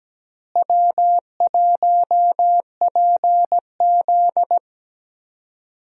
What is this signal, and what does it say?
Morse code, heard as a keyed tone, about 17 wpm, 700 Hz. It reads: W1PZ
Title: USA Call Signs - 4 Characters - Repeated 2x - 17wpm